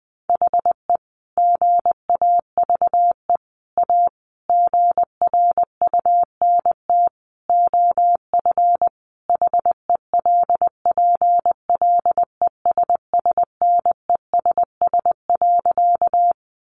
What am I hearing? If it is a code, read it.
Morse code, heard as a keyed tone, about 20 wpm, 700 Hz. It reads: HE GAVE A GRUNT OF HELPLESSNESS.
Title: A Sound Of Thunder 20wpm